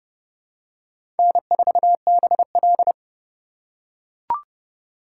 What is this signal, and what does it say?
Morse code, heard as a keyed tone, about 30 wpm, 700 Hz. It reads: N4BL
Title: USA Call Signs - 4 Characters - 3x Character Spacing - 30wpm